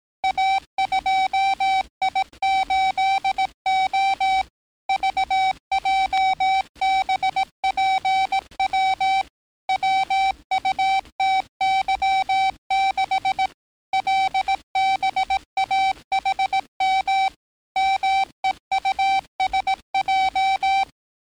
Morse code, heard as a keyed tone, about 18 wpm, 770 Hz. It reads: A2I8O VJBPW WUTY6 LBAHM MEUSJ
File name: training / NAVPERS 11049to11060 / NAVPERS 11058 RA RB